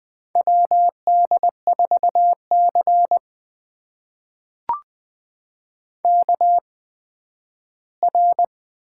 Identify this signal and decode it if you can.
Morse code, heard as a keyed tone, about 20 wpm, spaced as for 12 wpm, 700 Hz. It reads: WD4C KR
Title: USA Call Signs - 4 Characters - 8x Character Spacing - 20wpm